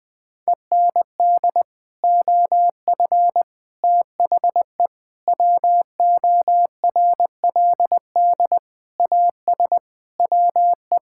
Morse code, heard as a keyed tone, about 21 wpm, 700 Hz. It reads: END OF THE WORLD AS WE
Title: Sets of 8 Words - 20wpm